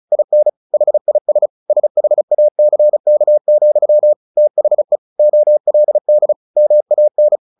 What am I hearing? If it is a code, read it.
Morse code, heard as a keyed tone, about 34 wpm, 600 Hz. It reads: IN HIS SHACK, THE OLD MAN
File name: the old man and the sea 2 35wpm